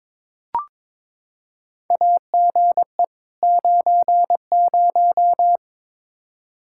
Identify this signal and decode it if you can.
Morse code, heard as a keyed tone, about 22 wpm, 700 Hz. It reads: AGE 90
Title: Practice QSO - Exchange 3 - More Info - 22wpm